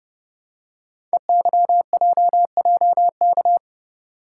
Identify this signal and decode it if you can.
Morse code, heard as a keyed tone, about 30 wpm, 700 Hz. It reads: EYJJK
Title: Code Groups - 5 Letters - 30wpm